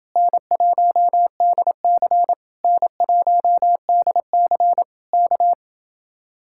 Morse code, heard as a keyed tone, about 29 wpm, 700 Hz. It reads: N1DC N1DC K